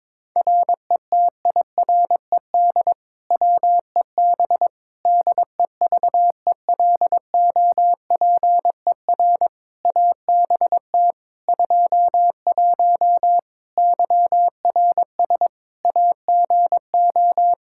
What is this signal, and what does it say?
Morse code, heard as a keyed tone, about 22 wpm, 700 Hz. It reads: RETIRED WEB DEVELOPER ABT 21 YRS AGO
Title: Practice QSO - Exchange 3 - More Info - 22wpm